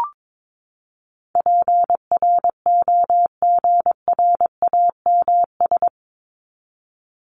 Morse code, heard as a keyed tone, about 22 wpm, 700 Hz. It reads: PROGRAMS
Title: ES:S Suffix - Top 100 Words with Base Words 22wpm